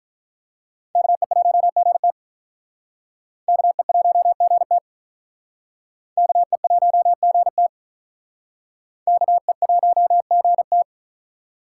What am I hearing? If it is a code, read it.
Morse code, heard as a keyed tone, about 45 wpm, 700 Hz. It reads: KE1GT KE1GT KE1GT KE1GT